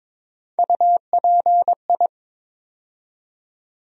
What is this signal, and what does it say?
Morse code, heard as a keyed tone, about 22 wpm, 700 Hz. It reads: UPI
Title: USA Call Sign Postfixes - 3 Letters 22wpm